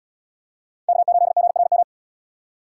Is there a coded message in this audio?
Yes: Morse code from 0.9 to 1.8 s, heard as a keyed tone, about 66 wpm, 700 Hz. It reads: K8GRM